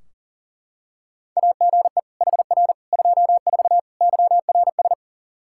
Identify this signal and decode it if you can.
Morse code, heard as a keyed tone, about 40 wpm, 700 Hz. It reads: AGE HR 24 YRS
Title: K1USN SST QSO Practice 40wpm